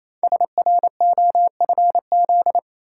Morse code, heard as a keyed tone, about 29 wpm, 700 Hz. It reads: SROFZ
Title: Code Groups - 5 Letters - 28wpm